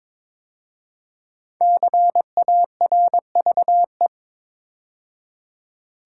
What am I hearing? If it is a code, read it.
Morse code, heard as a keyed tone, about 22 wpm, 700 Hz. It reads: CARVE
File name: 5 Letter Words - First 3 Letters Given - 22wpm